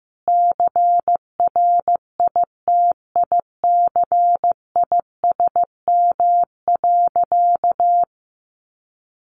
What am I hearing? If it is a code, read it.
Morse code, heard as a keyed tone, about 15 wpm, 700 Hz. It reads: CRITICISM.
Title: Sentences from Top 1300 Words 15wpm